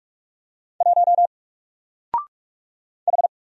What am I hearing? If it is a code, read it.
Morse code, heard as a keyed tone, about 44 wpm, 700 Hz. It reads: 1 H
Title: Single Character (Letter:Number:Punctuation:Prosign) Alt 45wpm